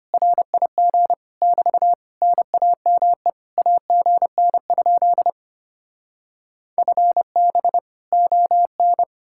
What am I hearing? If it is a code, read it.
Morse code, heard as a keyed tone, about 33 wpm, 700 Hz. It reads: RIG <BT> NAME AGN? FB ON